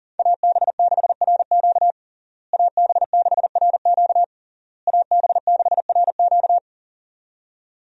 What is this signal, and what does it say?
Morse code, heard as a keyed tone, about 37 wpm, 700 Hz. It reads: AB6RQ AB6RQ AB6RQ
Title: USA Call Signs - 5 Characters - Repeated 3x - 40wpm